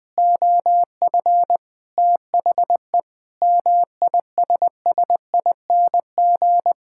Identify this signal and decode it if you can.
Morse code, heard as a keyed tone, about 21 wpm, 700 Hz. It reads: OF THE MISSING
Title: Sentences with 4-Word Phrases - 20wpm